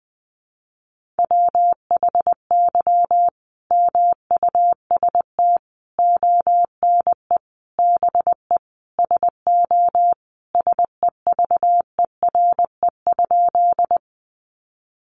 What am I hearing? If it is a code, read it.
Morse code, heard as a keyed tone, about 20 wpm, 700 Hz. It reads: WHY MUST ONE BE SO SEVERE?
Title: Sentences from Top 1400 Words 20wpm